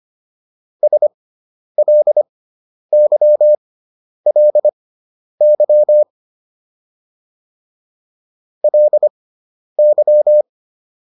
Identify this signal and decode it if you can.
Morse code, heard as a keyed tone, about 25 wpm, spaced as for 10 wpm, 600 Hz. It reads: SLYLY LY